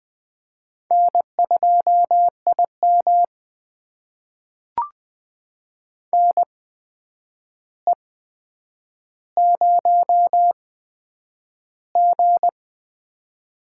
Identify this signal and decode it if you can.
Morse code, heard as a keyed tone, about 20 wpm, spaced as for 9 wpm, 700 Hz. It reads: N2IM NE0G
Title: USA Call Signs - 4 Characters - 8x Character Spacing - 20wpm